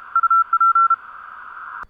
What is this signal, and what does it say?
Morse code, heard as a keyed tone, about 32 wpm, 1350 Hz. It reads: UP
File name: vk9dwx 30 cw